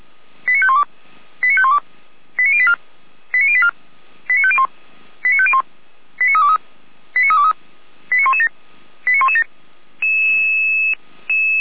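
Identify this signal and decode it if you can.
Five-tone selective call (ZVEI), heard as quick runs of five tones, tones of about 70 ms. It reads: 89512 89512 89084 89084 89501 89501 89323 89323 89117 89117
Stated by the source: Five-tone ZVEI selective call received by SDR.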